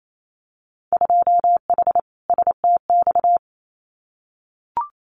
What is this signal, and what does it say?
Morse code, heard as a keyed tone, about 28 wpm, 700 Hz. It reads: 2H STX
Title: Winter Field Day Exchanges Encore 28wpm